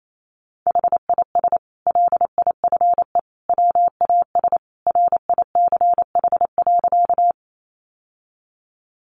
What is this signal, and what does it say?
Morse code, heard as a keyed tone, about 28 wpm, 700 Hz. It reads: HIS LIFE WAS RICH.
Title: Sentences from Top 1100 Words 28wpm